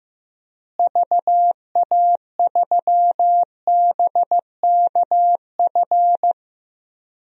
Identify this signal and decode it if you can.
Morse code, heard as a keyed tone, about 15 wpm, 700 Hz. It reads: VA3BKF
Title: DXCC-100 Call Signs - 6 Characters - Repeated 3x - 15wpm